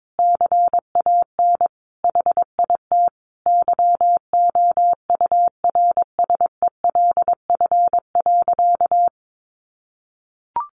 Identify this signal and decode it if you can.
Morse code, heard as a keyed tone, about 22 wpm, 700 Hz. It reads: CAN HIT YOURSELF.
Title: Sentences from Top 1200 Words 22wpm